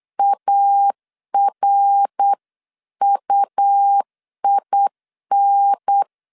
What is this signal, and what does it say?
Morse code, heard as a keyed tone, about 8 wpm, 800 Hz. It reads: ARUIN
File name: LEZIONI / 6 PKX0 / 6 PKX 8wpm 3041